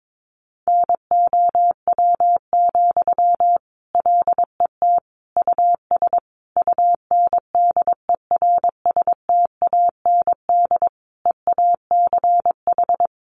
Morse code, heard as a keyed tone, about 22 wpm, 700 Hz. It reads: NOW, LET US UNDERSTAND EACH
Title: Sentences from Top 1500 Words 22wpm